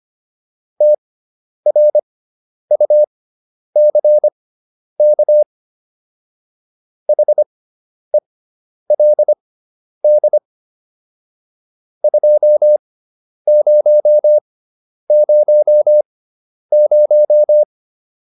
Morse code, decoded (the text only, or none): TRUCK HELD 2000